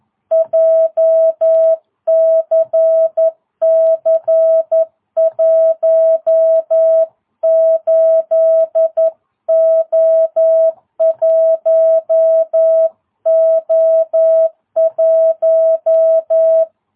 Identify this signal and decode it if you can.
Morse code, heard as a keyed tone, about 11 wpm, 660 Hz. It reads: JCC 18O1O1
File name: jh8cbh / 2022cw